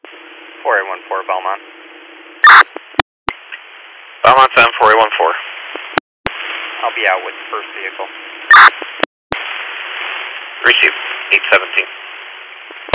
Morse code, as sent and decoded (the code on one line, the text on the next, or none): none